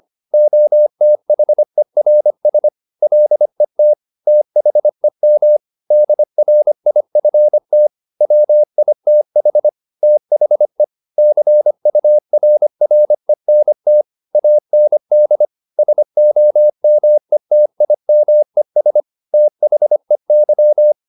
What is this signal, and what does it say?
Morse code, heard as a keyed tone, about 25 wpm, 600 Hz. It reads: OTHERS LET THEM DRIFT WITH THE CURRENT AND SOMETIMES THEY
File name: the old man and the sea part 1 25wpm